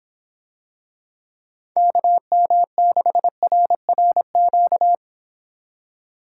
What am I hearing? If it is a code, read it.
Morse code, heard as a keyed tone, about 25 wpm, 700 Hz. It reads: KM6RRQ